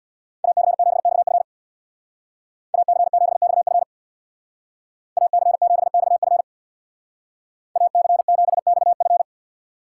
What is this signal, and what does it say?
Morse code, heard as a keyed tone, about 60 wpm, 700 Hz. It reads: AC7XF AC7XF AC7XF AC7XF
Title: Speed Racing - USA Call Signs 50wpm